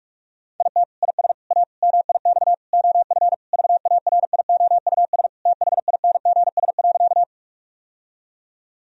Morse code, heard as a keyed tone, about 43 wpm, 700 Hz. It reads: IT IS A MIX OF VARIOUS THINGS.
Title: Sentences from Top 1000 Words - Review Part 4 - 45wpm